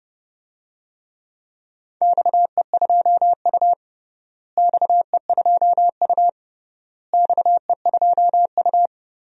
Morse code, heard as a keyed tone, about 30 wpm, 700 Hz. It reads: XE2U XE2U XE2U
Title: DXCC-100 Call Signs - 4 Characters - Repeated 3x - 30wpm